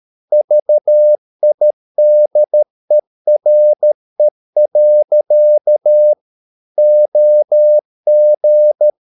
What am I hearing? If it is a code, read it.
Morse code, heard as a keyed tone, about 13 wpm, 600 Hz. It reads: VIDERE. OG
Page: Ferie 13 wpm